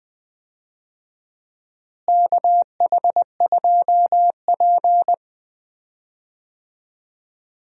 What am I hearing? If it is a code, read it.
Morse code, heard as a keyed tone, about 20 wpm, 700 Hz. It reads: KH2P
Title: USA Call Signs - 4 Characters - Repeated 2x - 20wpm